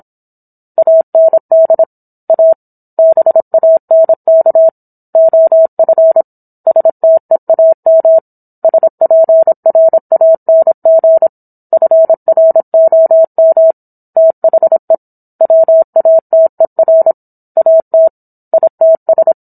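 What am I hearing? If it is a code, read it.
Morse code, heard as a keyed tone, about 27 wpm, 650 Hz. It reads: AND A BANK OF STEAM SPRANG FROM THE WATER AT ITS
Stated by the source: Morse code practice of a book chapter at 26 WPM.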